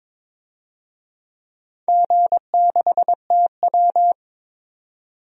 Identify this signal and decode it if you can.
Morse code, heard as a keyed tone, about 22 wpm, 700 Hz. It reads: G6TW
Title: DXCC-100 Call Signs - 4 Characters - 22wpm